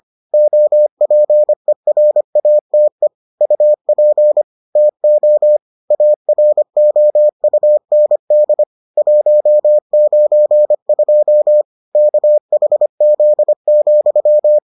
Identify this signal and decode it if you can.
Morse code, heard as a keyed tone, about 25 wpm, 600 Hz. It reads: OPERATE UP TO AROUND 192 KHZ,